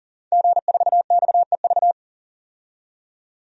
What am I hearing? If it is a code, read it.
Morse code, heard as a keyed tone, about 40 wpm, 700 Hz. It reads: G4XEV
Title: DXCC-100 Call Signs - 5 Characters - 40wpm